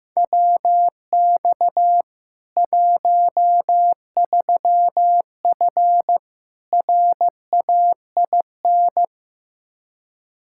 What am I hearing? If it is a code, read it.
Morse code, heard as a keyed tone, about 15 wpm, 700 Hz. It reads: WX 13F RAIN
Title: Practice QSO - Exchange 2 - Info - 15wpm